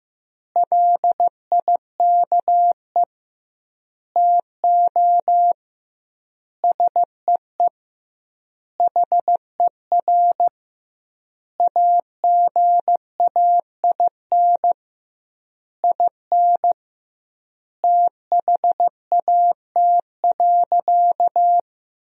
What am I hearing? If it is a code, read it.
Morse code, heard as a keyed tone, about 16 wpm, 700 Hz. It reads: LIKE TO SEE HER AGAIN IN THAT.
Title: 2x Word-Spacing - Sentences from Top 200 Words - 15wpm